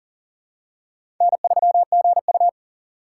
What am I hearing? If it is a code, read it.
Morse code, heard as a keyed tone, about 40 wpm, 700 Hz. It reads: N3GU
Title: USA Call Signs - 4 Characters - 2x Character Spacing - 40wpm